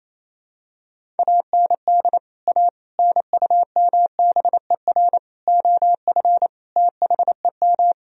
Morse code, heard as a keyed tone, about 28 wpm, 700 Hz. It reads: AND A NUMBER OF THEM
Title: Sentences from Top 1000 Words - Review Part 1 - 28wpm